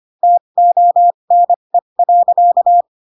Morse code, heard as a keyed tone, about 25 wpm, spaced as for 22 wpm, 700 Hz. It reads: TONE.
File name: Pinocchio / Pinocchio ch7 18 25 ik1hge